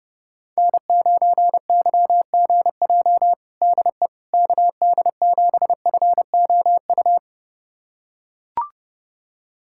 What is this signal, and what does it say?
Morse code, heard as a keyed tone, about 32 wpm, 700 Hz. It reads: N9YGJ DE KD7FOU
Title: USA Callsign DE Callsign 30wpm